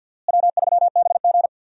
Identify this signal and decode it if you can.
Morse code, heard as a keyed tone, about 49 wpm, 700 Hz. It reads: W3BG
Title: USA Call Signs - 4 Characters - Repeated 2x - 50wpm